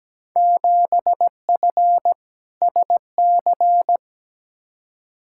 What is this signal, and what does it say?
Morse code, heard as a keyed tone, about 17 wpm, 700 Hz. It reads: 7F SC
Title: Field Day Exchanges Encore - 17wpm